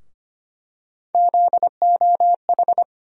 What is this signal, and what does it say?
Morse code, heard as a keyed tone, about 25 wpm, 700 Hz. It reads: ZOH
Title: USA Call Sign Postfixes - 3 Letters 25wpm